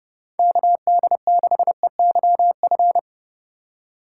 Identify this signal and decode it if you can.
Morse code, heard as a keyed tone, about 30 wpm, 700 Hz. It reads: KD6EYF
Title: USA Call Signs - 6 Characters - 2x Character Spacing - 30wpm